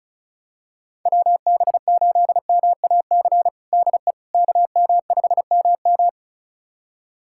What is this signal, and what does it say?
Morse code, heard as a keyed tone, about 36 wpm, 700 Hz. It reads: WB8MAC DE KM5MM
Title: USA Callsign DE Callsign 35wpm